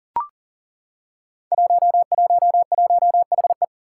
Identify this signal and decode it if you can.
Morse code, heard as a keyed tone, about 42 wpm, 700 Hz. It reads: 111HE
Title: Five Challenging Letter-Number Combinations 40wpm